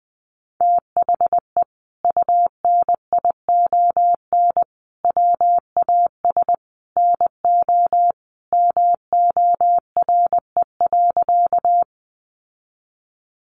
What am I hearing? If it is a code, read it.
Morse code, heard as a keyed tone, about 20 wpm, 700 Hz. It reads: THE UNION WAS NO MORE.
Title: Sentences from Top 1400 Words 20wpm